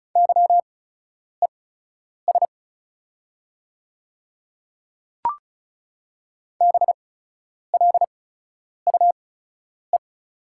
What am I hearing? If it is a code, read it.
Morse code, heard as a keyed tone, about 36 wpm, spaced as for 10 wpm, 690 Hz. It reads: YES BLUE
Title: 8x Character Spacing - Top 500 Words - 35wpm